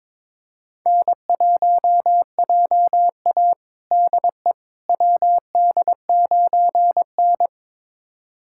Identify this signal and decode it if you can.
Morse code, heard as a keyed tone, about 21 wpm, 700 Hz. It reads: N1JA DE WD9N
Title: USA Callsign DE Callsign - 4 Characters - 22wpm